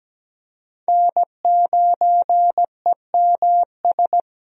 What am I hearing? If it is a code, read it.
Morse code, heard as a keyed tone, about 17 wpm, 700 Hz. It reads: N9EMS
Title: USA Call Signs - 5 Characters - 17wpm